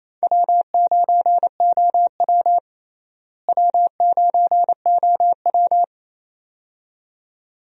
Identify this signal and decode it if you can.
Morse code, heard as a keyed tone, about 27 wpm, 700 Hz. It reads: W9OW W9OW
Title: USA Call Signs - 4 Characters - Repeated 2x - 28wpm